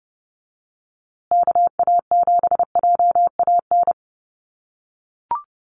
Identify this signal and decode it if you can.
Morse code, heard as a keyed tone, about 30 wpm, 700 Hz. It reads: KA7JAN
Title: USA Call Signs Repeated 2x 30wpm